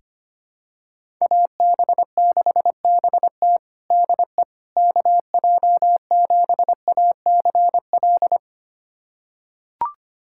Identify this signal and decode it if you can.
Morse code, heard as a keyed tone, about 25 wpm, 700 Hz. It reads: AB6BT DE KJ7ACL
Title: USA Callsign DE Callsign 25wpm